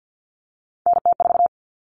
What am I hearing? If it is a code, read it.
Morse code, heard as a keyed tone, about 50 wpm, 700 Hz. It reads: NT4
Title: Common Cut Numbers - 3-Digit 50wpm